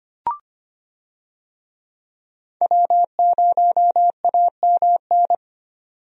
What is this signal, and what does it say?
Morse code, heard as a keyed tone, about 25 wpm, 700 Hz. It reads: W0AMN
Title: USA Call Signs - 5 Characters - 25wpm